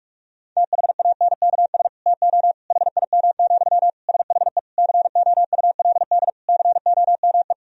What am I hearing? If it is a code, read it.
Morse code, heard as a keyed tone, about 46 wpm, 700 Hz. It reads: THANKS TO HIM, SHE COULD COME
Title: Sentences from Top 1000 Words - Review Part 3 - 45wpm